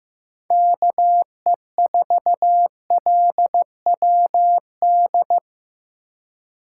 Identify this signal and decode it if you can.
Morse code, heard as a keyed tone, about 15 wpm, 700 Hz. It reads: KE4LWD
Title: USA Call Signs - 6 Characters - 4x Character Spacing - 15wpm